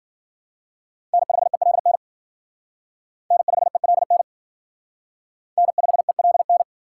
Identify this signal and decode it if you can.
Morse code, heard as a keyed tone, about 47 wpm, 700 Hz. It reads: N5ELN N5ELN N5ELN
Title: Speed Racing - USA 1x3 Call Signs 40wpm